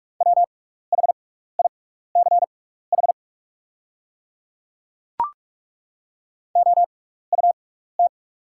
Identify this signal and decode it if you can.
Morse code, heard as a keyed tone, about 43 wpm, spaced as for 15 wpm, 700 Hz. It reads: WHICH OUT
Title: 6x Character Spacing - Top 300 Words - 45wpm